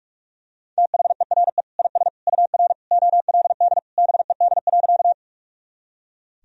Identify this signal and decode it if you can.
Morse code, heard as a keyed tone, about 42 wpm, 700 Hz. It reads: THERE IS UR OLD BED.
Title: Sentences from Top 500 Words and Abbreviations 45wpm